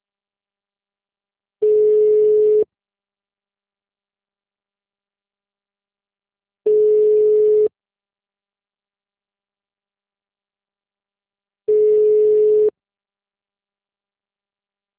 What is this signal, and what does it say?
Ringback tone (Europe, 425 Hz).
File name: out- -2001- - - .70737